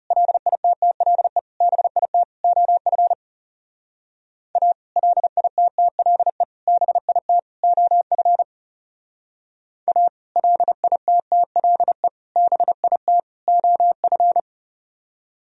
Morse code, heard as a keyed tone, about 36 wpm, 700 Hz. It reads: LITTLE BIT OF A LITTLE BIT OF A LITTLE BIT OF
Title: Speed-Racing - Sets of 4 Words 30wpm